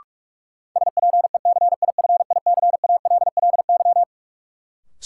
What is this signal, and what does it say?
Morse code, heard as a keyed tone, about 46 wpm, 700 Hz. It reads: SPECIFICALLY
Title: Sentences from Top 1100 Words - New Word Drill Alt - 45wpm